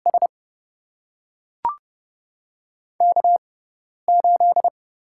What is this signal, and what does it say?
Morse code, heard as a keyed tone, about 30 wpm, spaced as for 10 wpm, 700 Hz. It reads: S K8
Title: USA Call Signs - 5 Characters - 6x Character Spacing - 30wpm